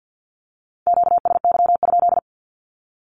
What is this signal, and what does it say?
Morse code, heard as a keyed tone, about 50 wpm, 700 Hz. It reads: QSY?
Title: Q-Codes Repeated 3x 50wpm